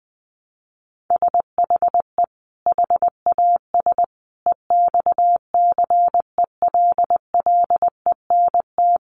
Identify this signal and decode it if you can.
Morse code, heard as a keyed tone, about 20 wpm, 700 Hz. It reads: SHE HAS EXCELLENT
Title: Sentences from Top 1100 Words 20wpm